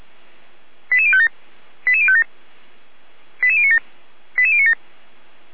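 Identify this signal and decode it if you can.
Five-tone selective call (ZVEI), heard as quick runs of five tones, tones of about 70 ms. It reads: 88057 88057 88097 88097
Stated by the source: ZVEI five-tone selective call received on an SDR.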